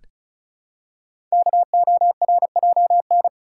Morse code, heard as a keyed tone, about 36 wpm, 700 Hz. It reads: KORJN
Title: Code Groups - 5 Letters - 35wpm